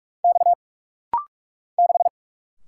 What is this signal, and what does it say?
Morse code, heard as a keyed tone, about 44 wpm, 690 Hz. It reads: X 6
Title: Single Letter-Number - Rapid-Fire 45wpm